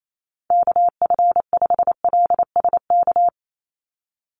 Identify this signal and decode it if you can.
Morse code, heard as a keyed tone, about 28 wpm, 700 Hz. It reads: KF5LSK
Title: USA Call Signs 28wpm